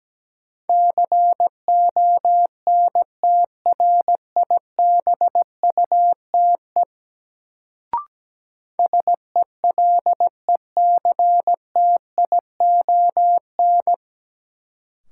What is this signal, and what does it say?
Morse code, heard as a keyed tone, about 17 wpm, 700 Hz. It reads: CONTRIBUTE SELECTION
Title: Sentences from Top 1400 Words - New Word Drill Alt - 17wpm